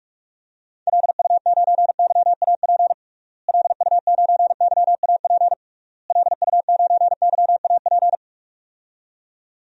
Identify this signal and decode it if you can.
Morse code, heard as a keyed tone, about 46 wpm, 700 Hz. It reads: LU9YAP LU9YAP LU9YAP
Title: DXCC-100 Call Signs - 6 Characters - Repeated 3x - 45wpm